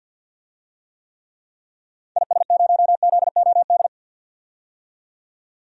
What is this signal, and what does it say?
Morse code, heard as a keyed tone, about 52 wpm, 700 Hz. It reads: IS0ZOD
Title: DXCC-100 Call Signs - 6 Characters - 50wpm